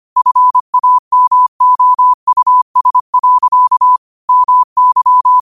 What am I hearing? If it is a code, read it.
Morse code, heard as a keyed tone, about 25 wpm, 1000 Hz. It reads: FAMOUS. MY
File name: 25wpm